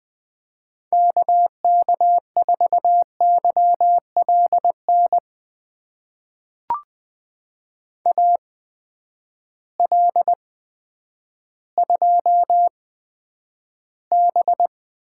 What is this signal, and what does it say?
Morse code, heard as a keyed tone, about 20 wpm, spaced as for 10 wpm, 700 Hz. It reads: KK4YLN AL2B
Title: USA Call Signs - 8x Character Spacing - 20wpm